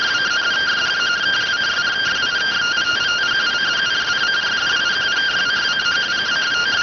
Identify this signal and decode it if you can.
Two-tone teleprinter signal, about 170 Hz shift.